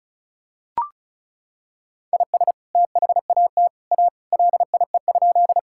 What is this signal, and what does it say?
Morse code, heard as a keyed tone, about 34 wpm, 700 Hz. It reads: IS THAT A LIE?
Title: Sentences from Top 1600 Words 35wpm